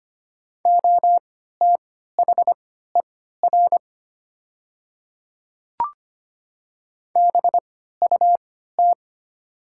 Morse code, heard as a keyed tone, about 25 wpm, spaced as for 14 wpm, 700 Hz. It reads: OTHER BUT
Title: 3x Character Spacing - Top 100 Words - 25wpm